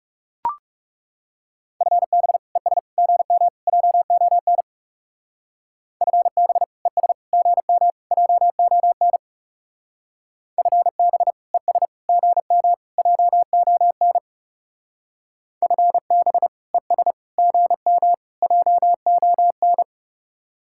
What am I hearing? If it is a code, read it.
Morse code, heard as a keyed tone, about 37 wpm, 700 Hz. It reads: FB ES GM JON FB ES GM JON FB ES GM JON FB ES GM JON
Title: Speed-Racing Common QSO Phrases - 30wpm